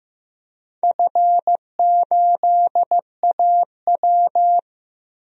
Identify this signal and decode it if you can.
Morse code, heard as a keyed tone, about 15 wpm, 700 Hz. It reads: F8AW
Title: DXCC-100 Call Signs - 4 Characters - 15wpm